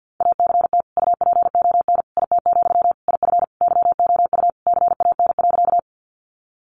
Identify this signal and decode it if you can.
Morse code, heard as a keyed tone, about 48 wpm, 700 Hz. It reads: ACT UPON IT, IF YOU CAN.
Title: Sentences from Top 1100 Words 50wpm